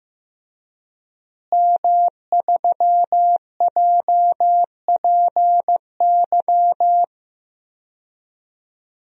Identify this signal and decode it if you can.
Morse code, heard as a keyed tone, about 15 wpm, 700 Hz. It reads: M3JPY
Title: DXCC-100 Call Signs - 5 Characters - Repeated 3x - 15wpm